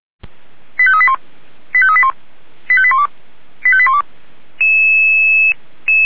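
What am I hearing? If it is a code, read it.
Five-tone selective call (ZVEI), heard as quick runs of five tones, tones of about 80 ms. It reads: 85381 85381 85812 85812
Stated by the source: SDR-received ZVEI five-tone selective call.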